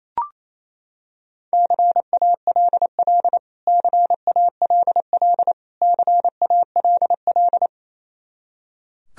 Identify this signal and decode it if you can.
Morse code, heard as a keyed tone, about 28 wpm, 700 Hz. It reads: CALL CALL CALL
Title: Top 100 Words Repeated 28wpm